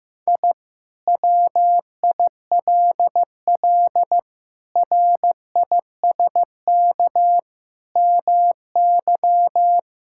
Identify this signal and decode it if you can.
Morse code, heard as a keyed tone, about 15 wpm, 700 Hz. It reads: I WILL RISK MY
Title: Sentences from Top 1000 Words - Review Part 5 - 15wpm